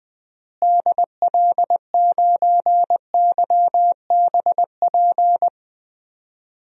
Morse code, heard as a keyed tone, about 20 wpm, 700 Hz. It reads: DL9YBP
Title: DXCC-100 Call Signs - 6 Characters - Repeated 2x - 20wpm